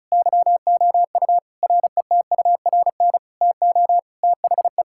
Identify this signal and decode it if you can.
Morse code, heard as a keyed tone, about 36 wpm, 700 Hz. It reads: YOU RETURN TO THE
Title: Sentences from Top 1600 Words 35wpm